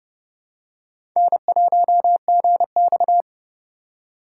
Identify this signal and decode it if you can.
Morse code, heard as a keyed tone, about 29 wpm, 700 Hz. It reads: N1GX
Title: USA Call Signs - 3x Character Spacing - 30wpm